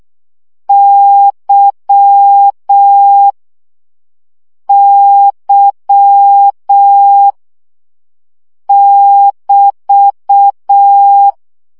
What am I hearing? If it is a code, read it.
Morse code, heard as a keyed tone, about 6 wpm, 800 Hz. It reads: Y Y <BT>
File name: CW gyakorlas y 5 baud